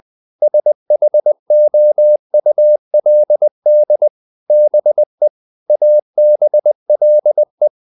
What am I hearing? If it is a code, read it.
Morse code, heard as a keyed tone, about 20 wpm, 600 Hz. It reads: SHOULD BE ABLE